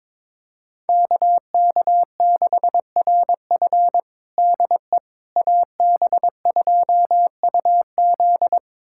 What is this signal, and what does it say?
Morse code, heard as a keyed tone, about 22 wpm, 700 Hz. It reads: KK6RF DE AB2UZ
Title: USA Callsign DE Callsign - 5 Characters - 22wpm